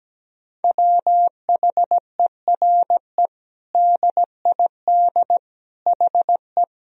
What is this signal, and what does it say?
Morse code, heard as a keyed tone, about 17 wpm, 700 Hz. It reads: WHERE DID HE
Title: Sentences from Top 200 Words and Abbreviations 17wpm